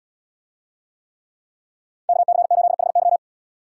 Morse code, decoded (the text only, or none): KC8HY